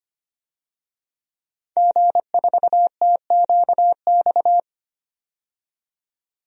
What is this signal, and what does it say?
Morse code, heard as a keyed tone, about 25 wpm, 700 Hz. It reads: G4TQX
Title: DXCC-100 Call Signs - 5 Characters - 25wpm